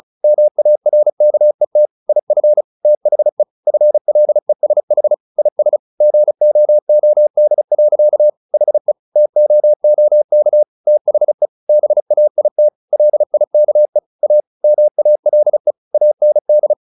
Morse code, heard as a keyed tone, about 34 wpm, 600 Hz. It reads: MARKET IF THE FLESH IS GOOD. HE TOOK THE BAIT LIKE A MALE AND
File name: the old man and the sea / the old man and the sea 1 35wpm